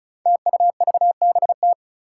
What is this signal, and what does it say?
Morse code, heard as a keyed tone, about 36 wpm, 700 Hz. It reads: TUVBT
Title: Five Challenging Letter-Number Combinations 35wpm